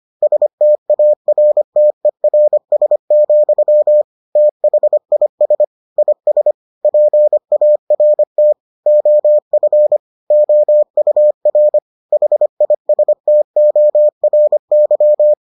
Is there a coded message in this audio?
Yes: Morse code from 0.2 to 15.3 s, heard as a keyed tone, about 25 wpm, 600 Hz. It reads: STARTERS, THIS IS PART OF OUR HISTORY